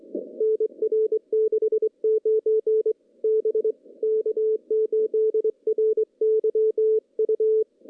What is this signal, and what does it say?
Morse code, heard as a keyed tone, about 21 wpm, 410 Hz. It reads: NR 69 B K8RYU
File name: contests / CW-SS-k5tr- -0200